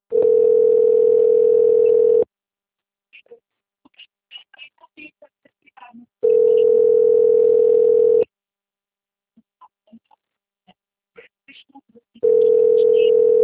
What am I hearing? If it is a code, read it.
Ringback tone (North America), 440 + 480 Hz.